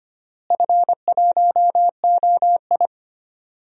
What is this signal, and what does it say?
Morse code, heard as a keyed tone, about 25 wpm, 700 Hz. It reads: F1OI
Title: DXCC-100 Call Signs - 4 Characters - Repeated 2x - 25wpm